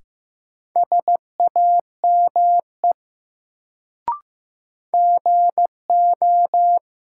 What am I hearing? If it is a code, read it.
Morse code, heard as a keyed tone, about 15 wpm, 700 Hz. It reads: SAME GO
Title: Sets of 2 Words - New Word Drill - 15wpm